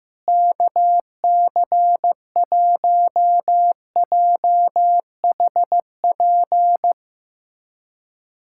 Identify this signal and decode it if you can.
Morse code, heard as a keyed tone, about 15 wpm, 700 Hz. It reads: KC1JHP
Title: USA Callsign DE Callsign 15wpm